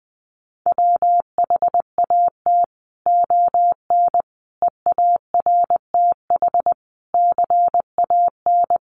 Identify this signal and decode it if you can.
Morse code, heard as a keyed tone, about 20 wpm, 700 Hz. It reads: WHAT ON EARTH CAN
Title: Sentences from Top 700 Words 20wpm